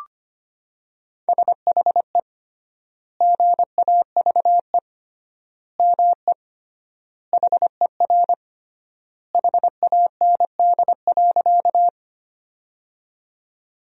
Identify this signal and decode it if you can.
Morse code, heard as a keyed tone, about 25 wpm, 700 Hz. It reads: SHE GAVE ME HER HAND.
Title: 3x Word-Spacing - Sentences from Top 500 Words - 25wpm